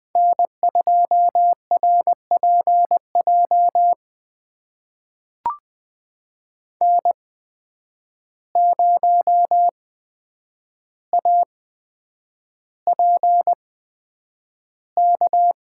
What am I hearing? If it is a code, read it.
Morse code, heard as a keyed tone, about 20 wpm, spaced as for 9 wpm, 700 Hz. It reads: N2RPJ N0APK
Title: USA Call Signs - 5 Characters - 8x Character Spacing - 20wpm